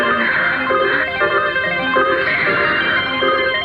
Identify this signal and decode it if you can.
Emergency Alert System (SAME) tones, heard as harsh buzzing bursts. It is heard from 0.2 to 3.1 s.